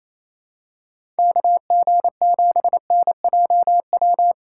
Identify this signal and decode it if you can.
Morse code, heard as a keyed tone, about 28 wpm, 700 Hz. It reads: KG7NJW
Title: USA Call Signs - 2x Character Spacing - 28wpm